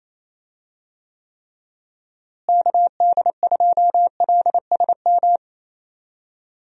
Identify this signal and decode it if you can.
Morse code, heard as a keyed tone, about 27 wpm, 700 Hz. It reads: KD2LSM
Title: USA Call Signs - 6 Characters - Repeated 3x - 28wpm